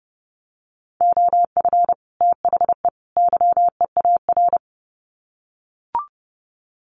Morse code, heard as a keyed tone, about 30 wpm, 700 Hz. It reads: OF THE YEAR
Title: Sets of 3 Words 30wpm